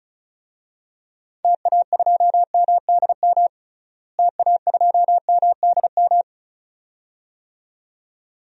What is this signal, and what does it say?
Morse code, heard as a keyed tone, about 34 wpm, 700 Hz. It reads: TA2MDM TA2MDM
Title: DXCC-100 Call Signs - 6 Characters - Repeated 2x - 35wpm